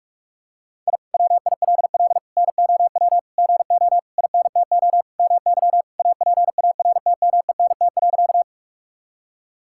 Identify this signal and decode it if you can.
Morse code, heard as a keyed tone, about 42 wpm, 700 Hz. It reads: I WILL NOW GO INTO MY APARTMENT.
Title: Sentences from Top 1000 Words - Review Part 6 - 45wpm